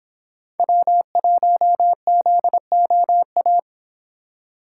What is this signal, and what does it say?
Morse code, heard as a keyed tone, about 28 wpm, 700 Hz. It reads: W1ZOA